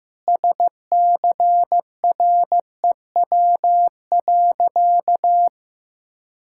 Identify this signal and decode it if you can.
Morse code, heard as a keyed tone, about 15 wpm, 700 Hz. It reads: SCREW.
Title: Sentences from Top 1600 Words 15wpm